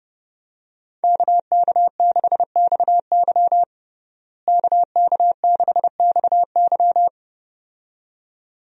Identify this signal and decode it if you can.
Morse code, heard as a keyed tone, about 30 wpm, 700 Hz. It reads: KK6XY KK6XY
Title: USA Call Signs - 5 Characters - Repeated 2x - 30wpm